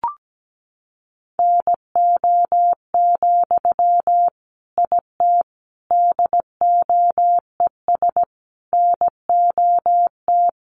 Morse code, heard as a keyed tone, about 17 wpm, 700 Hz. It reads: NO, IT DOES NOT
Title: Sentences from Top 1000 Words 17wpm